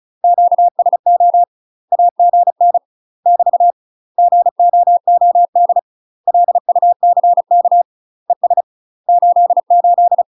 Morse code, heard as a keyed tone, about 36 wpm, 700 Hz. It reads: QSO AGN <BT> GOOD LUCK ES 88
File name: UK MASTER QSOs TRAINING / 35WPM / UK MASTER QSOs 0031